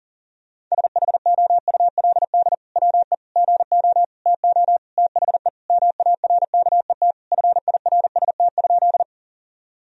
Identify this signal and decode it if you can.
Morse code, heard as a keyed tone, about 40 wpm, 700 Hz. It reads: SHOULD WE GO TO THE MARKET FIRST?
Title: Sentences from Top 1000 Words - Review Part 3 - 40wpm